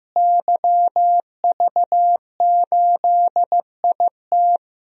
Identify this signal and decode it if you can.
Morse code, heard as a keyed tone, about 15 wpm, 700 Hz. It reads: YV8IT
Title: DXCC-100 Call Signs - 5 Characters - Repeated 3x - 15wpm